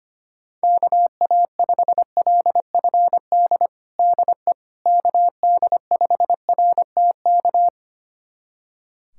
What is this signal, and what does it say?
Morse code, heard as a keyed tone, about 25 wpm, 700 Hz. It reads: KA5LFD DE KD5RTK
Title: USA Callsign DE Callsign - 6 Characters - 25wpm